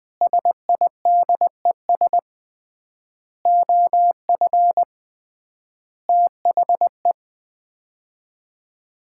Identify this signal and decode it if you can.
Morse code, heard as a keyed tone, about 20 wpm, 700 Hz. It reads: SIDES OF THE
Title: 3x Word-Spacing - Sets of 5 Words - 20wpm